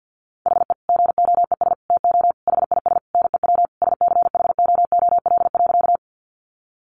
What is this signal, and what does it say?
Morse code, heard as a keyed tone, about 48 wpm, 700 Hz. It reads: HE GOES TO HIS NEW SCHOOL.
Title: Easy Phrases 50wpm